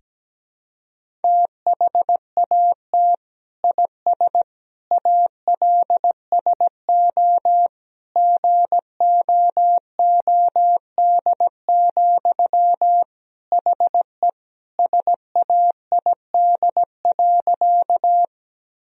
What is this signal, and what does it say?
Morse code, heard as a keyed tone, about 17 wpm, 700 Hz. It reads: THAT IS ALSO GOOD, HE SAID.
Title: Sentences from Top 1000 Words - Review Part 1 - 17wpm